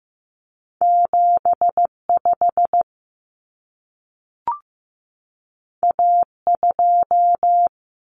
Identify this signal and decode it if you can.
Morse code, heard as a keyed tone, about 15 wpm, 700 Hz. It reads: 75 A2
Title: Common Cut Numbers - 2-Digit 15wpm